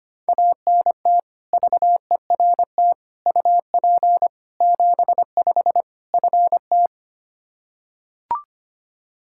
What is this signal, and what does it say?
Morse code, heard as a keyed tone, about 25 wpm, 700 Hz. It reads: ANT VERT UP 75 FT
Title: Practice QSO - Exchange 2 - Info - 25wpm